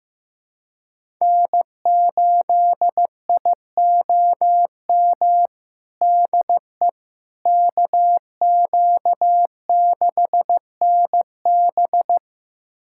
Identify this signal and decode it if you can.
Morse code, heard as a keyed tone, about 15 wpm, 700 Hz. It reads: N8IOM DE KQ6NB
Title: USA Callsign DE Callsign - 5 Characters - 15wpm